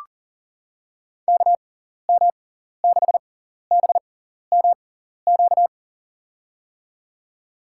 Morse code, heard as a keyed tone, about 40 wpm, spaced as for 14 wpm, 700 Hz. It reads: KM6BMQ
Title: USA Call Signs - 6 Characters - 6x Character Spacing - 40wpm